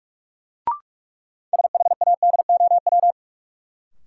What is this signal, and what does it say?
Morse code, heard as a keyed tone, about 43 wpm, 700 Hz. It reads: SHADOW
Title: Sets of 5 Words Encore - New Word Drill - 45wpm